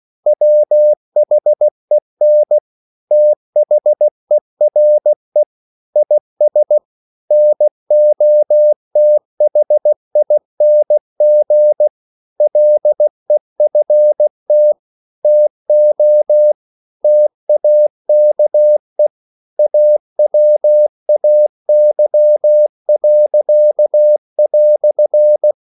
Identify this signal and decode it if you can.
Morse code, heard as a keyed tone, about 16 wpm, 600 Hz. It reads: WHEN THERE IS NOTHING LEFT TO TAKE AWAY."
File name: CW / Practice / EnglishQuotations0216